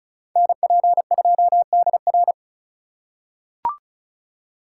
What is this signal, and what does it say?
Morse code, heard as a keyed tone, about 36 wpm, 700 Hz. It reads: NP2DR
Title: USA Call Signs - 4x Character Spacing - 35wpm